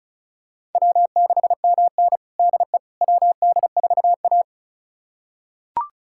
Morse code, heard as a keyed tone, about 34 wpm, 700 Hz. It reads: W6MN DE WD4A
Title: USA Callsign DE Callsign - 4 Characters - 35wpm